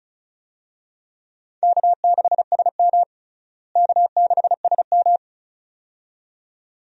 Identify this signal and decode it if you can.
Morse code, heard as a keyed tone, about 34 wpm, 700 Hz. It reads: K6SM K6SM
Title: USA Call Signs - 4 Characters - Repeated 2x - 35wpm